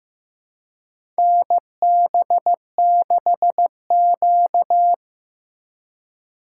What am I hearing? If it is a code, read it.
Morse code, heard as a keyed tone, about 15 wpm, 700 Hz. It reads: NB6Q
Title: USA Call Signs - 2x Character Spacing - 15wpm